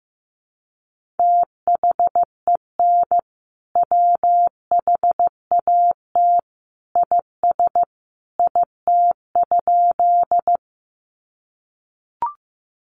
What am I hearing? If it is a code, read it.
Morse code, heard as a keyed tone, about 15 wpm, 700 Hz. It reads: THEN WHAT IS IT?
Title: Sentences from Top 100 Words 15wpm